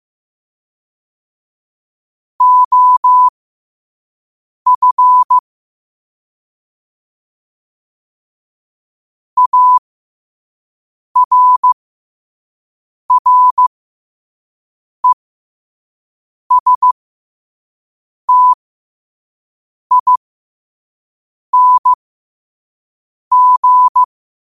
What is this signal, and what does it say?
Morse code, heard as a keyed tone, about 15 wpm, spaced as for 5 wpm, 1000 Hz. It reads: OF ARRESTING
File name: qotdfiles / 5wpm